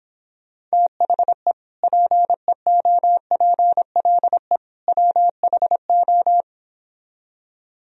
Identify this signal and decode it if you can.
Morse code, heard as a keyed tone, about 26 wpm, 700 Hz. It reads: THE PEOPLE WHO